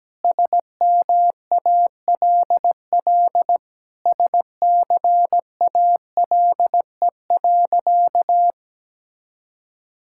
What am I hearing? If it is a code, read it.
Morse code, heard as a keyed tone, about 17 wpm, 700 Hz. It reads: SMALL SCALE.
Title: Sentences from Top 1000 Words - Review Part 5 - 17wpm